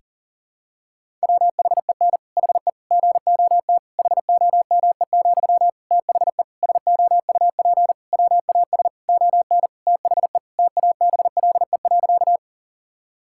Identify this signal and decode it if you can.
Morse code, heard as a keyed tone, about 40 wpm, 700 Hz. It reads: WHEN HE GOT HOME, THE SOUP WAS ON THE TABLE.
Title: Sentences from Top 1000 Words - Review Part 6 - 40wpm